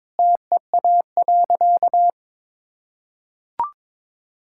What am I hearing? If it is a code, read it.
Morse code, heard as a keyed tone, about 22 wpm, 690 Hz. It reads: TEA.
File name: practice / 22wpm / Easy Sentences with Hints - 22wpm